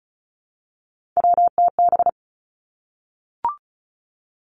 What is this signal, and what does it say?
Morse code, heard as a keyed tone, about 36 wpm, 700 Hz. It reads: WTB
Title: Three Challenging Letter-Number Combinations 35wpm